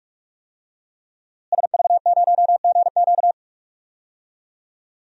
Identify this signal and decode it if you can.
Morse code, heard as a keyed tone, about 42 wpm, 700 Hz. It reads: SV0GQ
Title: DXCC-100 Call Signs - 5 Characters - Repeated 3x - 45wpm